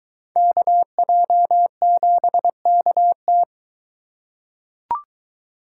Morse code, heard as a keyed tone, about 22 wpm, 700 Hz. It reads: KJ7KT